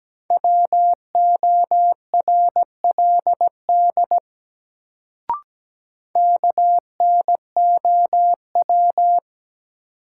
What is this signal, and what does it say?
Morse code, heard as a keyed tone, about 17 wpm, 700 Hz. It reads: WORLD KNOW
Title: Sets of 2 Words - New Word Drill - 17wpm